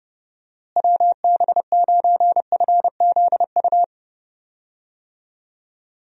Morse code, heard as a keyed tone, about 31 wpm, 700 Hz. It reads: WB9FZU
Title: USA Call Signs - 6 Characters - Repeated 2x - 30wpm